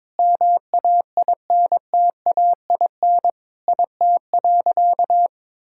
Morse code, heard as a keyed tone, about 21 wpm, 700 Hz. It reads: MAINTAIN IT.
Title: Sentences from Top 1000 Words - Review Part 5 - 22wpm